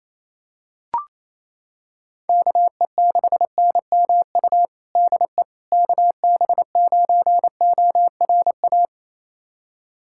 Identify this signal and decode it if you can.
Morse code, heard as a keyed tone, about 28 wpm, 700 Hz. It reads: KE6NMU DE KB9ORA
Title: USA Callsign DE Callsign 28wpm